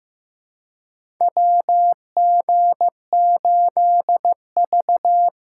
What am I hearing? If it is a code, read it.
Morse code, heard as a keyed tone, about 15 wpm, 700 Hz. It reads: WG8V
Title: USA Call Signs - 4 Characters - 2x Character Spacing - 15wpm